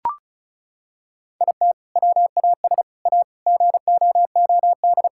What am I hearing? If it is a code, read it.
Morse code, heard as a keyed tone, about 36 wpm, 700 Hz. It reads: IT WAS A GOOD
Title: Sentences from Top 1000 Words - Review Part 5 - 35wpm